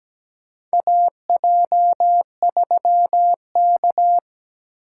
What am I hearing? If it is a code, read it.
Morse code, heard as a keyed tone, about 17 wpm, 700 Hz. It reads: AJ3K
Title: USA Call Signs - 4 Characters - 2x Character Spacing - 17wpm